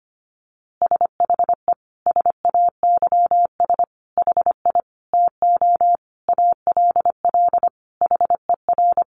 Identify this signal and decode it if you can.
Morse code, heard as a keyed tone, about 25 wpm, 700 Hz. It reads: SHE SAYS HI TO ALL HER
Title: Easy Phrases 25wpm